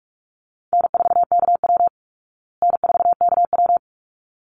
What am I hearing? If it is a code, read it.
Morse code, heard as a keyed tone, about 44 wpm, 700 Hz. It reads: N4KW N4KW
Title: Easy Call Signs Repeated 2x - 45wpm